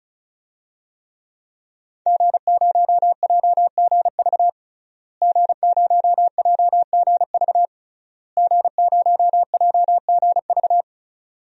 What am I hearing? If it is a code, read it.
Morse code, heard as a keyed tone, about 36 wpm, 700 Hz. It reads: G0JGV G0JGV G0JGV
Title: DXCC-100 Call Signs - 5 Characters - Repeated 3x - 35wpm